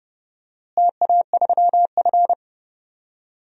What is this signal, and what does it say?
Morse code, heard as a keyed tone, about 30 wpm, 700 Hz. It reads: TA3F
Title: DXCC-100 Call Signs - 4 Characters - Repeated 2x - 30wpm